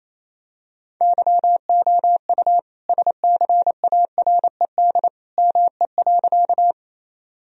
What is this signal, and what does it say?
Morse code, heard as a keyed tone, about 29 wpm, 700 Hz. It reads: YOU SCARED ME.
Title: Sentences from Top 1000 Words - Review Part 4 - 28wpm